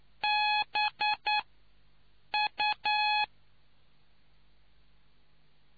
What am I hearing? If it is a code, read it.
Morse code, heard as a keyed tone, about 9 wpm, spaced as for 6 wpm, 840 Hz. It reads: BU